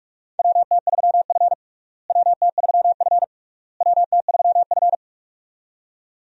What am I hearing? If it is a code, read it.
Morse code, heard as a keyed tone, about 42 wpm, 700 Hz. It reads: WT3F WT3F WT3F
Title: USA Call Signs - 4 Characters - Repeated 3x - 45wpm